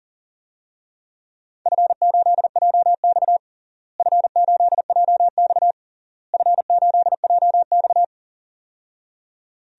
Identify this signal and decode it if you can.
Morse code, heard as a keyed tone, about 40 wpm, 700 Hz. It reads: F8JX F8JX F8JX
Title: DXCC-100 Call Signs - 4 Characters - Repeated 3x - 40wpm